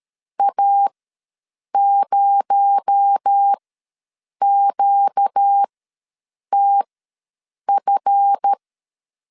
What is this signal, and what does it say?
Morse code, heard as a keyed tone, about 13 wpm, spaced as for 7 wpm, 800 Hz. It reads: A0QTF
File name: LEZIONI / 8 YJ45 / 8 Y ws 12wpm 3409